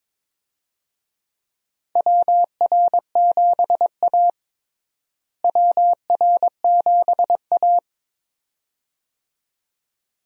Morse code, heard as a keyed tone, about 21 wpm, 700 Hz. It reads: WR7A WR7A
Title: USA Call Signs - 4 Characters - Repeated 2x - 22wpm